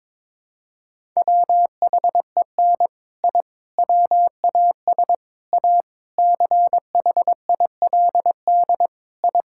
Morse code, heard as a keyed tone, about 22 wpm, 700 Hz. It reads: WHEN I WAS A CHILD I
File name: practice / 22wpm / Sentences with 3-Word Phrases - 22wpm